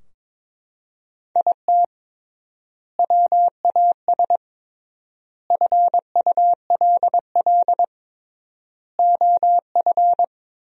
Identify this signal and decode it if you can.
Morse code, heard as a keyed tone, about 22 wpm, 700 Hz. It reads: IT WAS FULL OF
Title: 3x Word-Spacing - Sentences from Top 500 Words - 22wpm